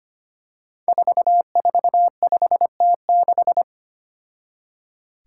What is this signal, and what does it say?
Morse code, heard as a keyed tone, about 25 wpm, 700 Hz. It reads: 445T6
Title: Five Challenging Letter-Number Combinations 25wpm